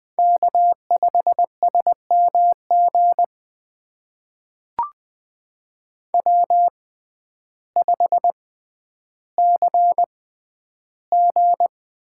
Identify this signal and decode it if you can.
Morse code, heard as a keyed tone, about 20 wpm, spaced as for 11 wpm, 700 Hz. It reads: K5SMG W5CG
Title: USA Call Signs - 5 Characters - 6x Character Spacing - 20wpm